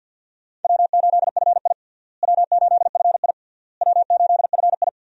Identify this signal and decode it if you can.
Morse code, heard as a keyed tone, about 52 wpm, 700 Hz. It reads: W8FI W8FI W8FI
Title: USA Call Signs - 4 Characters - Repeated 3x - 50wpm